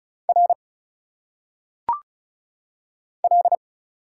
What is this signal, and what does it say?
Morse code, heard as a keyed tone, about 34 wpm, 700 Hz. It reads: R L
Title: Single Letters - Phonetic Pronunciation 35wpm